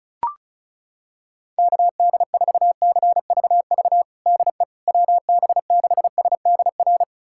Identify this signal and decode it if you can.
Morse code, heard as a keyed tone, about 36 wpm, 700 Hz. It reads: KD4CVV DE WB6SDR
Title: USA Callsign DE Callsign - 6 Characters - 35wpm